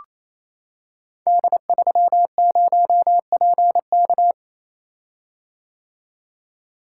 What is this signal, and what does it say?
Morse code, heard as a keyed tone, about 28 wpm, 700 Hz. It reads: D30PK
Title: Code Groups - 5 Letter:Numbers - 28wpm